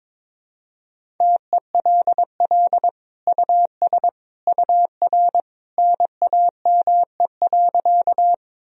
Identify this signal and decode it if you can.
Morse code, heard as a keyed tone, about 22 wpm, 700 Hz. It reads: TELL US UR NAME.
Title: Sentences from Top 200 Words and Abbreviations 22wpm